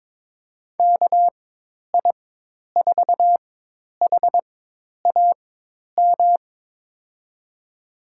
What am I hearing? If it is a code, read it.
Morse code, heard as a keyed tone, about 22 wpm, spaced as for 10 wpm, 700 Hz. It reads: KI4HAM
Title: USA Call Signs - 6 Characters - 4x Character Spacing - 22wpm